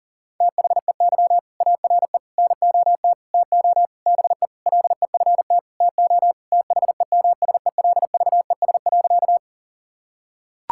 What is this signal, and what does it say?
Morse code, heard as a keyed tone, about 40 wpm, 700 Hz. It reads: THEY ARE NOT TO BE LEFT TO THEMSELVES.
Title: Sentences from Top 1000 Words - Review Part 5 - 40wpm